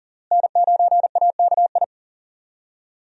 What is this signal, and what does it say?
Morse code, heard as a keyed tone, about 40 wpm, 700 Hz. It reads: N9AKI
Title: USA Call Signs - 5 Characters - Repeated 3x - 40wpm